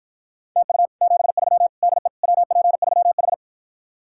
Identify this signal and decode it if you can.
Morse code, heard as a keyed tone, about 54 wpm, 700 Hz. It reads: TU 73 DE WP3H